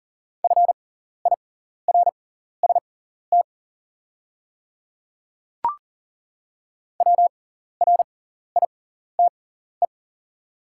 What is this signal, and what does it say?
Morse code, heard as a keyed tone, about 39 wpm, spaced as for 14 wpm, 690 Hz. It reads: FIRST WRITE
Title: 6x Character Spacing - Top 100 Words - 40wpm